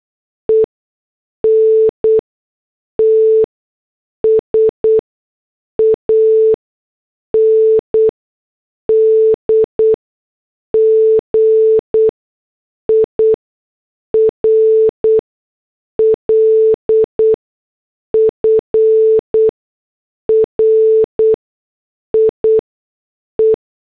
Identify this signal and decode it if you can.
Morse code, heard as a keyed tone, about 8 wpm, spaced as for 6 wpm, 440 Hz. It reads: ENTS AND GIRLFRIE